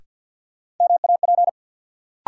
Morse code, heard as a keyed tone, about 51 wpm, 700 Hz. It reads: MAP
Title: Sentences from Top 400 Words - New Word Drill - 50wpm